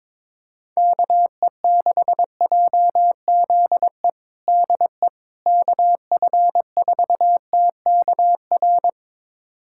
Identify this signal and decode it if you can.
Morse code, heard as a keyed tone, about 22 wpm, 700 Hz. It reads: KE6JZE DE KF4TKR
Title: USA Callsign DE Callsign - 6 Characters - 22wpm